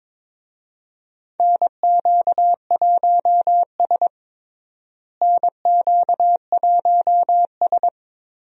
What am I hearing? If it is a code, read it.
Morse code, heard as a keyed tone, about 21 wpm, 700 Hz. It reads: NQ1S NQ1S
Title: USA Call Signs - 4 Characters - Repeated 2x - 22wpm